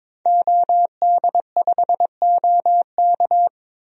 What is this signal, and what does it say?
Morse code, heard as a keyed tone, about 22 wpm, 700 Hz. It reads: OD5OK
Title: DXCC-100 Call Signs - 5 Characters - 22wpm